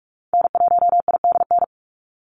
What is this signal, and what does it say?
Morse code, heard as a keyed tone, about 44 wpm, 700 Hz. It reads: N1IDN
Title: NA SOTA Activator Callsigns 45wpm